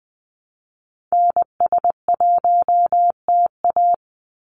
Morse code, heard as a keyed tone, about 20 wpm, 700 Hz. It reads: NS1TA
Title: NA SOTA Activator Callsigns 20wpm